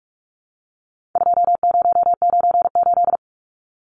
Morse code, heard as a keyed tone, about 44 wpm, 700 Hz. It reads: 2098
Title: Numbers - 4-Digit 45wpm